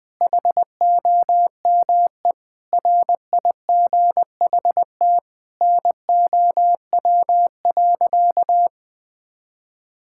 Morse code, heard as a keyed tone, about 21 wpm, 700 Hz. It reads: HOME RIGHT NOW.
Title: Sentences with 5-Word Phrases - 20wpm